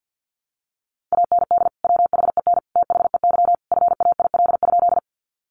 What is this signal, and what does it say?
Morse code, heard as a keyed tone, about 50 wpm, 700 Hz. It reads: AND WHEN THEY FAIL?
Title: Sentences from Top 1100 Words 50wpm